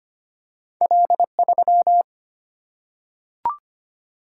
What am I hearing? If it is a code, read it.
Morse code, heard as a keyed tone, about 25 wpm, 700 Hz. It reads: L 3
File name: Two Letter-Number Combinations - 25wpm